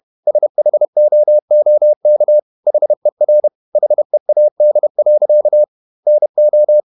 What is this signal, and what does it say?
Morse code, heard as a keyed tone, about 31 wpm, 600 Hz. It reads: SHOOK HER HEAD. NO
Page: Never 31wpm | CW med Gnister